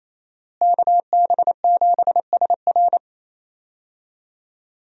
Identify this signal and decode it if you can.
Morse code, heard as a keyed tone, about 28 wpm, 700 Hz. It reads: KB7SR
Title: USA Call Signs - 5 Characters - Repeated 3x - 28wpm